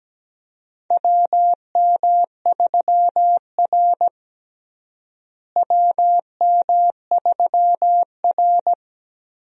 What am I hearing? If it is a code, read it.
Morse code, heard as a keyed tone, about 17 wpm, 700 Hz. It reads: WM3R WM3R
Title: USA Call Signs - 4 Characters - Repeated 3x - 17wpm